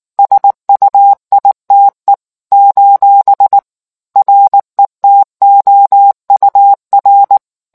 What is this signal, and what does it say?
Morse code, heard as a keyed tone, about 19 wpm, 800 Hz. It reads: SUITE : RETOUR